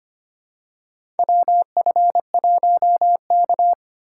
Morse code, heard as a keyed tone, about 25 wpm, 700 Hz. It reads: WF1K
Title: USA Call Signs - 4 Characters - 3x Character Spacing - 25wpm